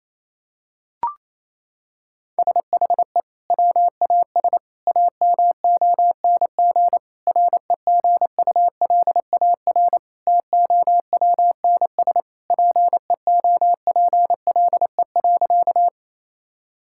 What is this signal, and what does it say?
Morse code, heard as a keyed tone, about 28 wpm, 700 Hz. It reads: SHE WAS AMONG REGULAR TOWNS PEOPLE.
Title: Sentences from Top 1000 Words - Review Part 6 - 28wpm